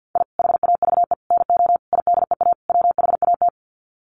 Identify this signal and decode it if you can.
Morse code, heard as a keyed tone, about 50 wpm, 700 Hz. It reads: I HAVE NO IDEA WHAT
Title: Sets of 5 Words Encore 50wpm